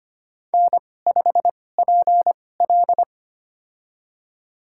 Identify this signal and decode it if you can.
Morse code, heard as a keyed tone, about 25 wpm, spaced as for 18 wpm, 700 Hz. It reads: N5PL
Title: USA Call Signs - 4 Characters - 2x Character Spacing - 25wpm